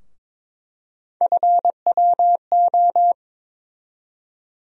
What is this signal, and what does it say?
Morse code, heard as a keyed tone, about 22 wpm, 700 Hz. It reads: FWO
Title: USA Call Sign Postfixes - 3 Letters 22wpm